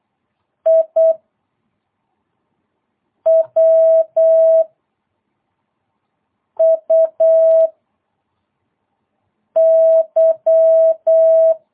Morse code, heard as a keyed tone, about 8 wpm, 650 Hz. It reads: I W U Y
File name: jh8cbh / 2022cw